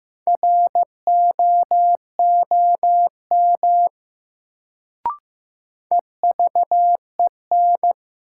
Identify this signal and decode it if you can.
Morse code, heard as a keyed tone, about 15 wpm, 700 Hz. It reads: ROOM EVEN
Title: Sets of 4 Words - New Word Drill - 15wpm